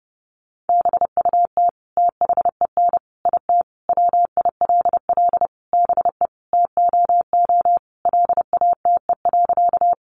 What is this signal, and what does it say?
Morse code, heard as a keyed tone, about 30 wpm, 700 Hz. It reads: BUT THEN IT WILL BE TOO LATE.
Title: Sentences from Top 300 Words 30wpm